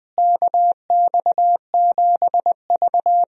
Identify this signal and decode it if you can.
Morse code, heard as a keyed tone, about 20 wpm, 700 Hz. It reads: KX7V
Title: USA Call Signs - 4 Characters - Repeated 3x - 20wpm